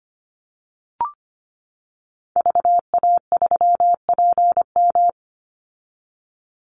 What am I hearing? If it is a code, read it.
Morse code, heard as a keyed tone, about 25 wpm, 700 Hz. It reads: VA3PM
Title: Call Signs - 25wpm